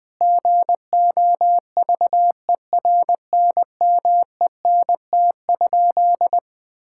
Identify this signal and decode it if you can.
Morse code, heard as a keyed tone, about 21 wpm, 700 Hz. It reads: GOVERNMENT?
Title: Sentences from Top 1000 Words - Review Part 5 - 20wpm